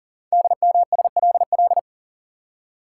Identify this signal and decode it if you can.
Morse code, heard as a keyed tone, about 39 wpm, 700 Hz. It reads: DMSLL
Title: Code Groups - 5 Letter:Numbers - 40wpm